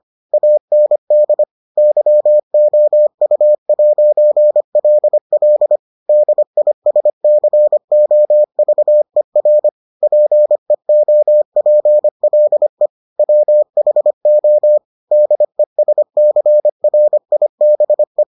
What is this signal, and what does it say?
Morse code, heard as a keyed tone, about 25 wpm, 600 Hz. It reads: AND YOU'LL DISCOVER PEOPLE WHO DESCRIBE